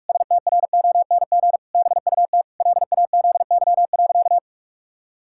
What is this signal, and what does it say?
Morse code, heard as a keyed tone, about 46 wpm, 700 Hz. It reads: STRONG BUT LAZY.
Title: Pangram Challenge v2 - 45wpm